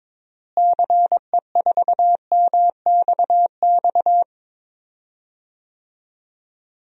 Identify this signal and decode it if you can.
Morse code, heard as a keyed tone, about 22 wpm, 700 Hz. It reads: CE4MXX
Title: DXCC-100 Call Signs - 6 Characters - Repeated 2x - 22wpm